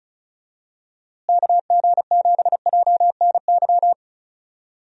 Morse code, heard as a keyed tone, about 36 wpm, 700 Hz. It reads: KG7JNY
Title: USA Call Signs - 6 Characters - Repeated 2x - 35wpm